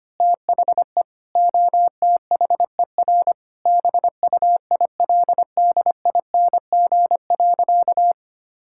Morse code, heard as a keyed tone, about 25 wpm, 700 Hz. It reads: THE OTHER BUILDING.
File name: practice / 25wpm / Sentences with 3-Word Phrases - 25wpm